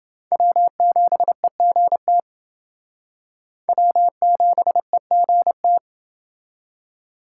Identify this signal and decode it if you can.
Morse code, heard as a keyed tone, about 28 wpm, 700 Hz. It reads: W7EGT W7EGT